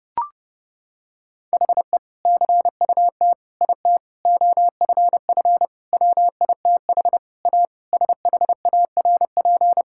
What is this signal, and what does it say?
Morse code, heard as a keyed tone, about 32 wpm, 700 Hz. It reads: HE CUT IT OFF WITH A SHARP
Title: Sentences from Top 1600 Words 30wpm